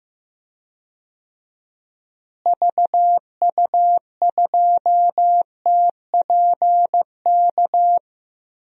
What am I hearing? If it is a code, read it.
Morse code, heard as a keyed tone, about 15 wpm, 700 Hz. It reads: VU2TPK
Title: DXCC-100 Call Signs - 6 Characters - Repeated 3x - 15wpm